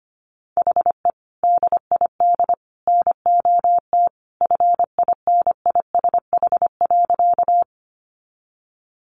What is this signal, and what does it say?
Morse code, heard as a keyed tone, about 25 wpm, 700 Hz. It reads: HE DID NOT FINISH.
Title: Sentences from Top 1200 Words 25wpm